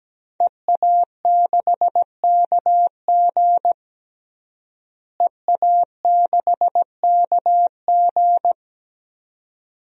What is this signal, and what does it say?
Morse code, heard as a keyed tone, about 17 wpm, 700 Hz. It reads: EA6KG EA6KG
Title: DXCC-100 Call Signs - 5 Characters - Repeated 2x - 17wpm